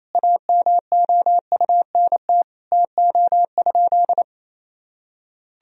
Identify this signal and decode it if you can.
Morse code, heard as a keyed tone, about 28 wpm, 700 Hz. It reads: AMOUNT TO?
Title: Sentences from Top 1000 Words - Review Part 4 - 28wpm